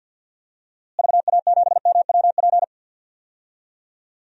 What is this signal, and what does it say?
Morse code, heard as a keyed tone, about 51 wpm, 700 Hz. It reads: VA7MWP
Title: DXCC-100 Call Signs - 6 Characters - Repeated 2x - 50wpm